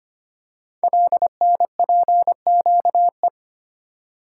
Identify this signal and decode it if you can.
Morse code, heard as a keyed tone, about 25 wpm, 700 Hz. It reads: LNPQE
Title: Code Groups - 5 Letters - 25wpm